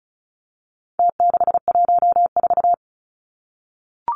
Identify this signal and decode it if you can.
Morse code, heard as a keyed tone, about 34 wpm, 700 Hz. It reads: T614
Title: Numbers - Contest Style - 4-Digit 35wpm